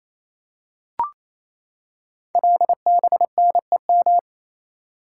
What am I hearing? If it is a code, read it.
Morse code, heard as a keyed tone, about 27 wpm, 700 Hz. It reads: LBNEM
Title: Code Groups - 5 Letters - 28wpm